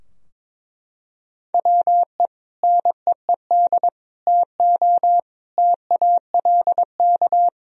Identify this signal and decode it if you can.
Morse code, heard as a keyed tone, about 22 wpm, 700 Hz. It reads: WE NEED TO TALK
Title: Sentences from Top 1000 Words - Review Part 4 - 22wpm